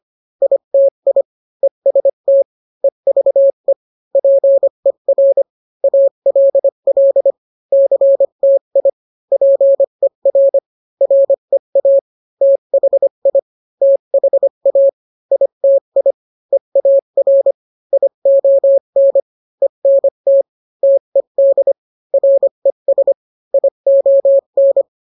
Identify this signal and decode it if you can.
Morse code, heard as a keyed tone, about 25 wpm, 550 Hz. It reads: ITI EST EVE PER ALL CTI PER REA THI THA ITI EAR ION ENT TED RES ION